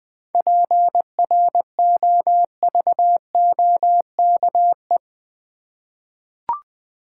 Morse code, heard as a keyed tone, about 20 wpm, 700 Hz. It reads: PROVOKE
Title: 7 Letter Words - First 3 Letters Given - 20wpm